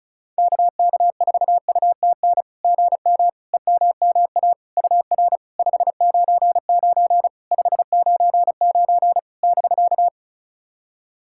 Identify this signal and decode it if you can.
Morse code, heard as a keyed tone, about 36 wpm, 700 Hz. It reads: KK4UTN GM EMMA UR 599 599 <BK>
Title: SOTA QSO Practice 35wpm